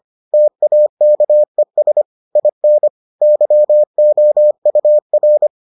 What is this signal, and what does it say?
Morse code, heard as a keyed tone, about 25 wpm, 600 Hz. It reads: TAKES IN YOUR